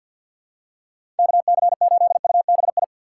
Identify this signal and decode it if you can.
Morse code, heard as a keyed tone, about 45 wpm, 700 Hz. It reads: KC8UBI
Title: USA Call Signs - 6 Characters - 3x Character Spacing - 50wpm